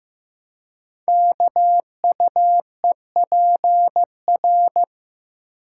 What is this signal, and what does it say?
Morse code, heard as a keyed tone, about 15 wpm, 700 Hz. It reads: KUEPR
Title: Code Groups - 5 Letters - 15wpm